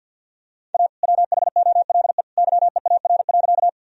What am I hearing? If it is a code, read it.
Morse code, heard as a keyed tone, about 48 wpm, 700 Hz. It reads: A WHOLE YEAR.
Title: Sentences from Top 1000 Words - Review Part 1 - 50wpm